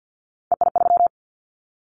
Morse code, heard as a keyed tone, about 50 wpm, 700 Hz. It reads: EI3
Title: Callsign Prefixes for Top 50 DXCC - 50wpm